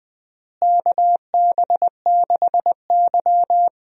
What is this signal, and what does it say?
Morse code, heard as a keyed tone, about 19 wpm, 700 Hz. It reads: KB6Y
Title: USA Call Signs - 4 Characters - Repeated 3x - 20wpm